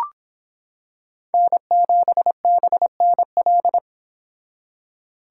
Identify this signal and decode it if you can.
Morse code, heard as a keyed tone, about 26 wpm, 700 Hz. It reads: N7BNL